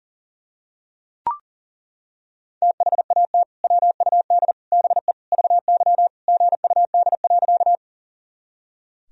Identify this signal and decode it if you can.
Morse code, heard as a keyed tone, about 40 wpm, 700 Hz. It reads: THAT WUD BE VY GUD.
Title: Sentences from Top 200 Words and Abbreviations 40wpm